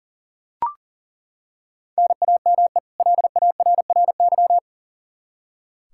Morse code, heard as a keyed tone, about 40 wpm, 700 Hz. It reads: NAME LARRY
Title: Practice QSO - Exchange 1 - Beginning - 40wpm